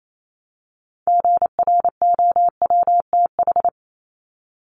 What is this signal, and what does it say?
Morse code, heard as a keyed tone, about 28 wpm, 700 Hz. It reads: GROWTH
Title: Top 751-1000 Words 28wpm